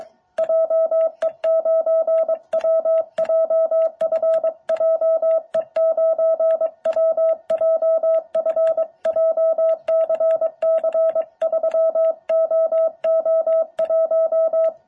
Morse code, heard as a keyed tone, about 22 wpm, 650 Hz. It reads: JE9WJF JE9WJF JCC3OO1
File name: jh8cbh / 2021cw